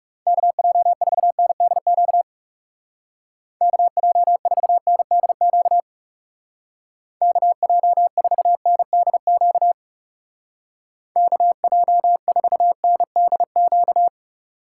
Morse code, heard as a keyed tone, about 36 wpm, 700 Hz. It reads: KJ4NDQ KJ4NDQ KJ4NDQ KJ4NDQ
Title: Speed Racing - USA Call Signs 30wpm